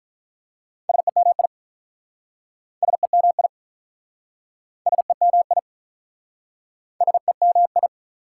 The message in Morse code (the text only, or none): SEMI SEMI SEMI SEMI